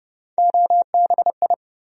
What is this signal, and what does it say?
Morse code, heard as a keyed tone, about 31 wpm, 700 Hz. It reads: OBI
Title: USA Call Sign Postfixes - 3 Letters 30wpm